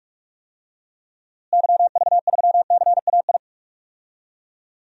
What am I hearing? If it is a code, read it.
Morse code, heard as a keyed tone, about 46 wpm, 700 Hz. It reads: YV3CAI
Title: DXCC-100 Call Signs - 6 Characters - Repeated 3x - 45wpm